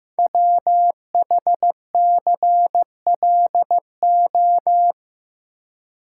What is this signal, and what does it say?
Morse code, heard as a keyed tone, about 15 wpm, 700 Hz. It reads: WHCLO
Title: Code Groups - 5 Letter:Numbers - 15wpm